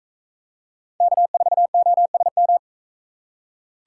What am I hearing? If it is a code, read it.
Morse code, heard as a keyed tone, about 41 wpm, 700 Hz. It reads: K4OSM